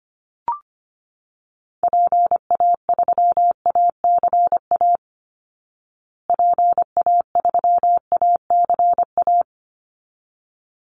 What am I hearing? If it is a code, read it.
Morse code, heard as a keyed tone, about 25 wpm, 700 Hz. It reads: PA3ACA PA3ACA
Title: Call Signs Repeated 3x - 25wpm